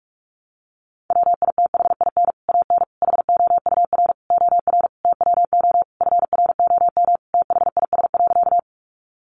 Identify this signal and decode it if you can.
Morse code, heard as a keyed tone, about 44 wpm, 700 Hz. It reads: WITHIN AN HOUR OR TWO FROM THIS.
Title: Sentences from Top 700 Words 45wpm